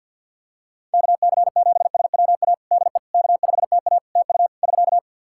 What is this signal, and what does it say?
Morse code, heard as a keyed tone, about 50 wpm, 700 Hz. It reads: KC7SWA DE K5TA TU <SK>
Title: Practice QSO - Exchange 4 - Ending - 50wpm